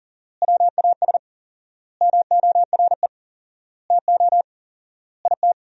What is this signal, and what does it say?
Morse code, heard as a keyed tone, about 40 wpm, 700 Hz. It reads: WAS MORE TO IT
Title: 4x Word-Spacing - Sentences from Top 100 Words - 40wpm